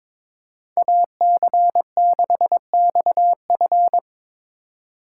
Morse code, heard as a keyed tone, about 22 wpm, 700 Hz. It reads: AC6XF
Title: USA Call Signs - 5 Characters - 3x Character Spacing - 22wpm